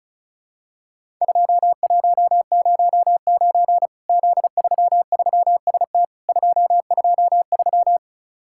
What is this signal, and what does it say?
Morse code, heard as a keyed tone, about 36 wpm, 700 Hz. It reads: 2109 Z33ST 223
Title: Worked All Europe DX Contest - QTC Exchange - 35wpm